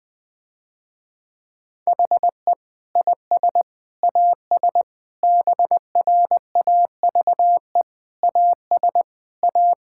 Morse code, heard as a keyed tone, about 21 wpm, 700 Hz. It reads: HE IS AS BRAVE AS A
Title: Difficult Sentences with Hints - 20wpm